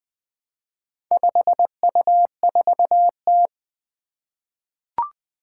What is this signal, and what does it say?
Morse code, heard as a keyed tone, about 20 wpm, 700 Hz. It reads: 5U4T
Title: Four Challenging Letter-Number Combinations 20wpm